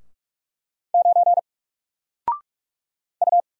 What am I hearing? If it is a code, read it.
Morse code, heard as a keyed tone, about 44 wpm, 700 Hz. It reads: 9 U
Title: Single Character (Letter:Number:Punctuation:Prosign) 45wpm